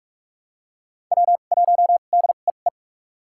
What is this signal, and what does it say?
Morse code, heard as a keyed tone, about 42 wpm, spaced as for 31 wpm, 700 Hz. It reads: W1DEE
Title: USA Call Signs - 2x Character Spacing - 45wpm